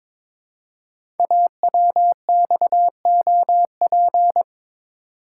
Morse code, heard as a keyed tone, about 22 wpm, 700 Hz. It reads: AWXOP
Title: Code Groups - 5 Letters - 22wpm